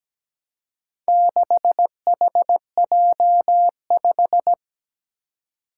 Morse code, heard as a keyed tone, about 17 wpm, 700 Hz. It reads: 6HJ5
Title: Four Challenging Letter-Number Combinations 17wpm